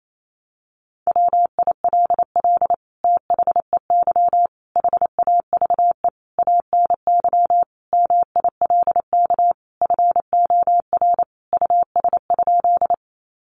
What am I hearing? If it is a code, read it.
Morse code, heard as a keyed tone, about 28 wpm, 700 Hz. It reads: WILL THEY HAVE ANY MILK FOR US?
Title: Sentences from Top 1500 Words 28wpm